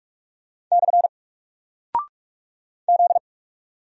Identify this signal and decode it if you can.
Morse code, heard as a keyed tone, about 44 wpm, 700 Hz. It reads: / Z
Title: Single Character (Letter:Number:Punctuation:Prosign) Alt 45wpm